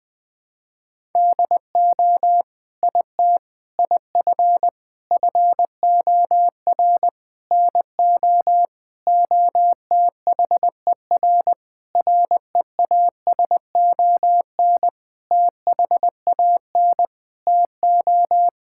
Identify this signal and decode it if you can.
Morse code, heard as a keyed tone, about 21 wpm, 700 Hz. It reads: DO IT IF FOR NO OTHER REASON THAN TO
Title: Sentences with 7-Word Phrases - Spoken Beginning - 20wpm